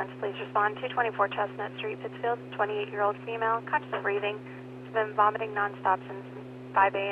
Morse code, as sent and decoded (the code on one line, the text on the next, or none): none